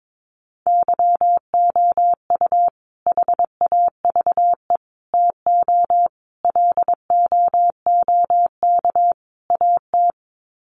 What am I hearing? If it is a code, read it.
Morse code, heard as a keyed tone, about 22 wpm, 700 Hz. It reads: YOU HAVE TO LOOK AT
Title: Sets of 5 Words Encore 22wpm